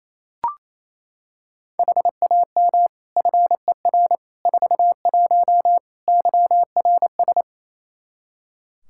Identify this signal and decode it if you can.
Morse code, heard as a keyed tone, about 27 wpm, 700 Hz. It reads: HAM FER 41 YRS
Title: Practice QSO - Exchange 3 - More Info - 28wpm